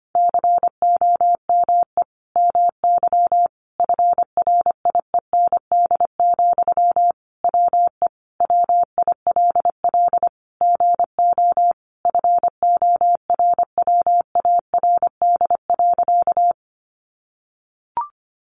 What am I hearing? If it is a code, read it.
Morse code, heard as a keyed tone, about 25 wpm, 700 Hz. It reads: COME MY FRIEND, WE WILL GO FORWARD.
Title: Sentences from Top 1300 Words 25wpm